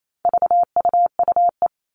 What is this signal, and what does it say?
Morse code, heard as a keyed tone, about 28 wpm, 700 Hz. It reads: VUUE
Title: All Cut Numbers - 4-Digit 28wpm